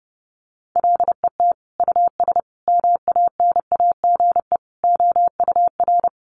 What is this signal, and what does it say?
Morse code, heard as a keyed tone, about 30 wpm, 700 Hz. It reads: LET US MANAGE OUR
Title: Sentences from Top 1400 Words 30wpm